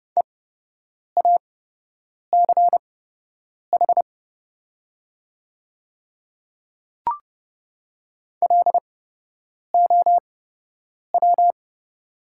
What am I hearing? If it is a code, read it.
Morse code, heard as a keyed tone, about 30 wpm, spaced as for 8 wpm, 700 Hz. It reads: EACH LOW
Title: 8x Character Spacing - Top 200 Words - 30wpm